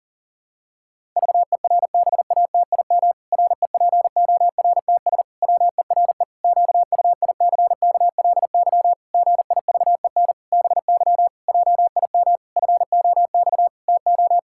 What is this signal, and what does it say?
Morse code, heard as a keyed tone, about 37 wpm, 700 Hz. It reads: VERBATIM REPORTS WERE QUICKLY GIVEN BY JIM FOX TO
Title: Pangram Challenge v2 - 40wpm